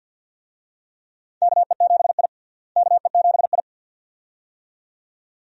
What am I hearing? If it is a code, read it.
Morse code, heard as a keyed tone, about 51 wpm, 700 Hz. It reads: KE7I KE7I
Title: USA Call Signs - 4 Characters - Repeated 2x - 50wpm